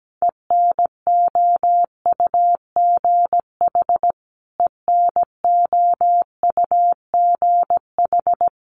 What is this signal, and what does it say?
Morse code, heard as a keyed tone, about 17 wpm, 700 Hz. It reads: ENOUGH ENOUGH
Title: Top 500 Words Repeated 17wpm